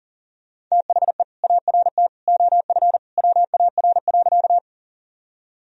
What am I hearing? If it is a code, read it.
Morse code, heard as a keyed tone, about 40 wpm, 700 Hz. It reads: THE ART OF WAR.
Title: Sentences from Top 1000 Words - Review Part 3 - 40wpm